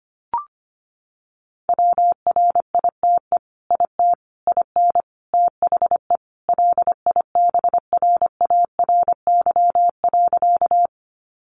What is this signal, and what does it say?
Morse code, heard as a keyed tone, about 25 wpm, 700 Hz. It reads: WRITE IT IN THE LIBRARY.
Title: Sentences from Top 600 Words 25wpm